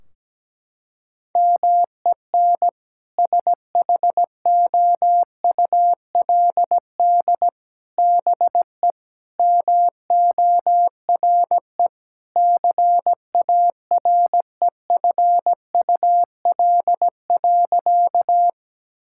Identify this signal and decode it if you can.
Morse code, heard as a keyed tone, about 17 wpm, 700 Hz. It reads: MEN SHOULD BE MORE CAREFUL.
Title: Sentences from Top 1600 Words 17wpm